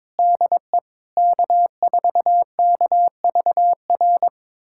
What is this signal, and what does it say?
Morse code, heard as a keyed tone, about 21 wpm, 700 Hz. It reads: DE K4KVR
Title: USA Callsign DE Callsign - 5 Characters - 22wpm